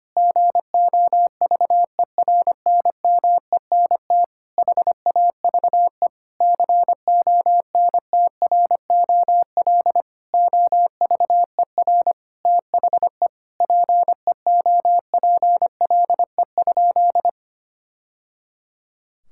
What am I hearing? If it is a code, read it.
Morse code, heard as a keyed tone, about 25 wpm, 700 Hz. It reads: GOVERNMENT HAVE CONTROL OVER THE PEOPLE?
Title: Sentences from Top 1000 Words - Review Part 5 - 25wpm